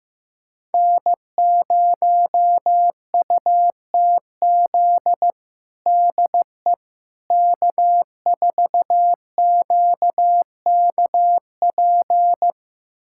Morse code, heard as a keyed tone, about 15 wpm, 700 Hz. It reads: N0UTZ DE K4QKP
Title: USA Callsign DE Callsign - 5 Characters - 15wpm